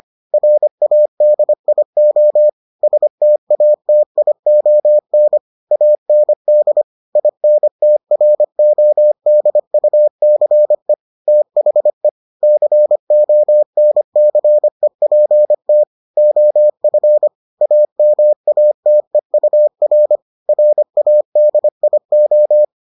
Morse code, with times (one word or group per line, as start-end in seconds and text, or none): 0.3-2.5 RADIO
2.8-5.4 STATION
5.7-6.8 AND
7.2-10.9 INTRODUCE
11.3-12.1 THE
12.4-15.8 CONCEPT
16.2-17.3 OF
17.6-20.2 AMATEUR
20.5-22.6 RADIO